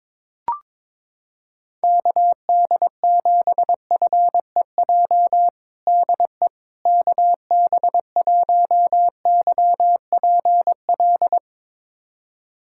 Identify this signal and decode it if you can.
Morse code, heard as a keyed tone, about 21 wpm, 700 Hz. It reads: KD7FEJ DE KB1YPL
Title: USA Callsign DE Callsign - 6 Characters - 22wpm